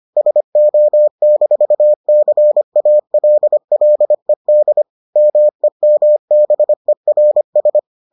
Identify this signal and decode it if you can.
Morse code, heard as a keyed tone, about 25 wpm, 600 Hz. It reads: SO-CALLED MEMBERS